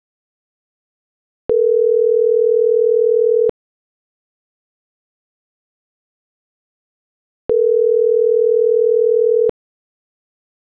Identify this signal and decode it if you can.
Ringback tone (North America), 440 + 480 Hz.